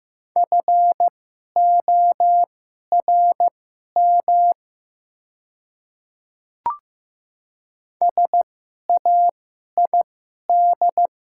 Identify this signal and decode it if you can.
Morse code, heard as a keyed tone, about 15 wpm, spaced as for 11 wpm, 700 Hz. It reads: FORM SAID
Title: 2x Character Spacing - Top 200 Words - 15wpm